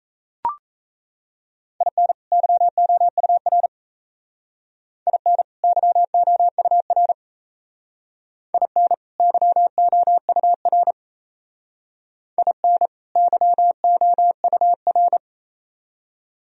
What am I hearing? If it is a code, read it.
Morse code, heard as a keyed tone, about 36 wpm, 700 Hz. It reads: IN YOUR IN YOUR IN YOUR IN YOUR